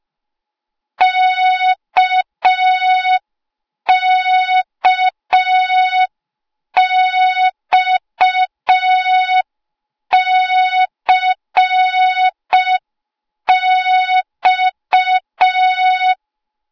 Morse code, heard as a keyed tone, about 5 wpm, 3700 Hz. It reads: KKXCX
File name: Morze-HA2MN / hang / 08KXC HA2MN